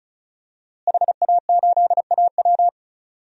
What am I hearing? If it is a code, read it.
Morse code, heard as a keyed tone, about 36 wpm, 700 Hz. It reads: HA8AW
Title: DXCC-100 Call Signs - 5 Characters - Repeated 3x - 35wpm